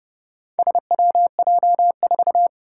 Morse code, heard as a keyed tone, about 31 wpm, 700 Hz. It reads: SWJ4
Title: Four Challenging Letter-Number Combinations 30wpm